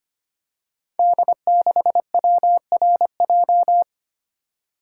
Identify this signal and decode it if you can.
Morse code, heard as a keyed tone, about 25 wpm, 700 Hz. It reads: D6WRJ
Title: Code Groups - 5 Letter:Numbers - 25wpm